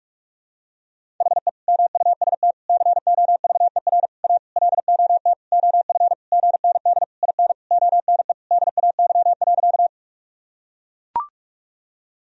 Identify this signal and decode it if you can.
Morse code, heard as a keyed tone, about 42 wpm, 700 Hz. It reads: HE MUST COVER A LOT OF GND IN ONE DAY.
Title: Sentences from Top 500 Words and Abbreviations 45wpm